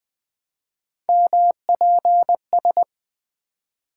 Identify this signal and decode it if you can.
Morse code, heard as a keyed tone, about 20 wpm, 700 Hz. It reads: MPS
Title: Three Letter-Number Combinations - 20wpm